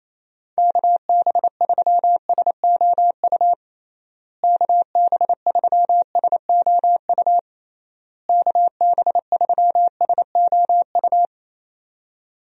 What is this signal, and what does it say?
Morse code, heard as a keyed tone, about 28 wpm, 700 Hz. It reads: KB3SOU KB3SOU KB3SOU
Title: USA Call Signs - 6 Characters - Repeated 3x - 28wpm